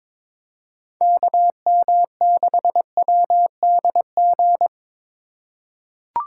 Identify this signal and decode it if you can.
Morse code, heard as a keyed tone, about 22 wpm, 700 Hz. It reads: KM6WDG
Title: USA Call Signs - 6 Characters - 2x Character Spacing - 22wpm